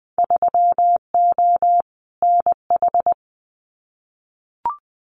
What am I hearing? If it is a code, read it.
Morse code, heard as a keyed tone, about 20 wpm, 700 Hz. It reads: 3O NH
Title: Winter Field Day Exchanges Encore 20wpm